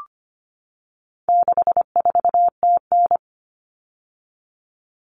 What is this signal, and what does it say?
Morse code, heard as a keyed tone, about 25 wpm, 700 Hz. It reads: 64TN
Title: All Cut Numbers - 4-Digit 25wpm